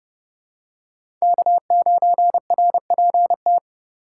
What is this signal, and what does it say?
Morse code, heard as a keyed tone, about 30 wpm, 700 Hz. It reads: K9RPT
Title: USA Call Signs - 2x Character Spacing - 30wpm